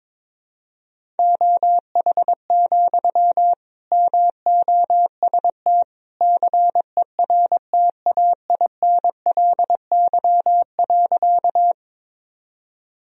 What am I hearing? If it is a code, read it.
Morse code, heard as a keyed tone, about 21 wpm, 700 Hz. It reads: OH, MOST CERTAINLY.
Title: Sentences from Top 1000 Words - Review Part 1 - 22wpm